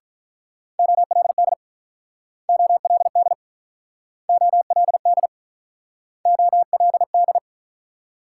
Morse code, heard as a keyed tone, about 46 wpm, 700 Hz. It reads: OLD OLD OLD OLD